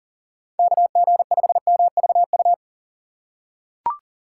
Morse code, heard as a keyed tone, about 40 wpm, 700 Hz. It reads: KG5MVU
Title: USA Call Signs - 6x Character Spacing - 40wpm